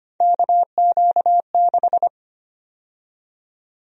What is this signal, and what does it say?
Morse code, heard as a keyed tone, about 25 wpm, 700 Hz. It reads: KQ6
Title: USA Call Sign Prefixes - Repeated 2x - 25wpm